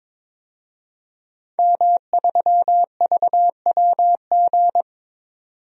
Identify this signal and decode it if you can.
Morse code, heard as a keyed tone, about 22 wpm, 700 Hz. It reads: M3VWG
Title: DXCC-100 Call Signs - 5 Characters - 22wpm